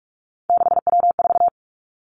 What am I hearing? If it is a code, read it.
Morse code, heard as a keyed tone, about 46 wpm, 700 Hz. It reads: 6W4
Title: Three Challenging Letter-Number Combinations 45wpm